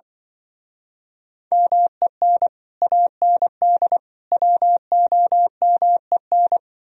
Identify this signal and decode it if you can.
Morse code, heard as a keyed tone, about 23 wpm, 700 Hz. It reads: MEN AND WOMEN
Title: Speed-Racing - Sets of 3 Words 20wpm